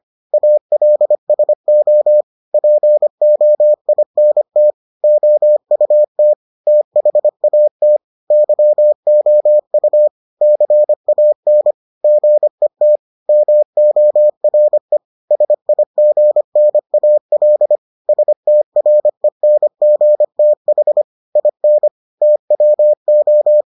Morse code, heard as a keyed tone, about 25 wpm, 600 Hz. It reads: ALSO POINT OUT THAT YOU CAN GET MORE SIGNAL STRENGTH IN TWO